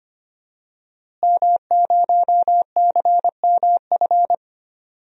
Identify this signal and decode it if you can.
Morse code, heard as a keyed tone, about 25 wpm, 700 Hz. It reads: M0CMF
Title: DXCC-100 Call Signs - 5 Characters - Repeated 2x - 25wpm